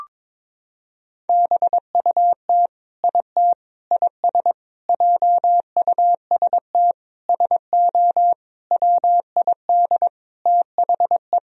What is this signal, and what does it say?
Morse code, heard as a keyed tone, about 22 wpm, 700 Hz. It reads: BUT IT IS JUST SO WID THE
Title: Sentences from Top 200 Words and Abbreviations 22wpm